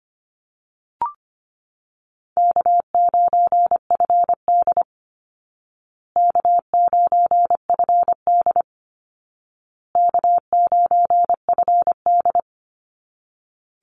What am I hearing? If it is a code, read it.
Morse code, heard as a keyed tone, about 25 wpm, 700 Hz. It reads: K9FD K9FD K9FD
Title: Easy Call Signs Repeated 3x - 25wpm